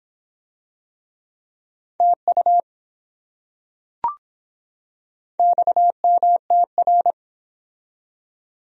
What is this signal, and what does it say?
Morse code, heard as a keyed tone, about 24 wpm, 700 Hz. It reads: TU XMTR